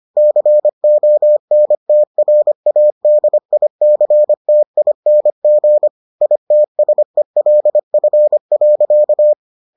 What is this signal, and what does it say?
Morse code, heard as a keyed tone, about 25 wpm, 600 Hz. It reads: CONTRADICTING ITSELF.